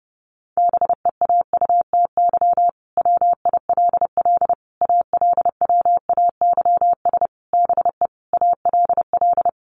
Morse code, heard as a keyed tone, about 30 wpm, 700 Hz. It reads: BEAUTY WILL ALWAYS BE ALL
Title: Sentences from Top 1300 Words 30wpm